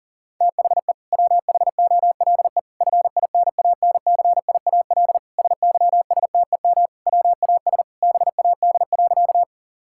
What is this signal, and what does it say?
Morse code, heard as a keyed tone, about 40 wpm, 700 Hz. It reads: THE WHOLE FINANCIAL SYSTEM WAS BAD.
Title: Sentences from Top 1000 Words - Review Part 5 - 40wpm